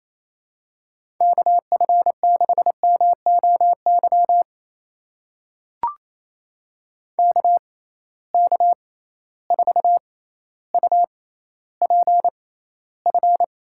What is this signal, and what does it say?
Morse code, heard as a keyed tone, about 28 wpm, spaced as for 14 wpm, 700 Hz. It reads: KF6MOY KK4UPF
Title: USA Call Signs - 6 Characters - 6x Character Spacing - 28wpm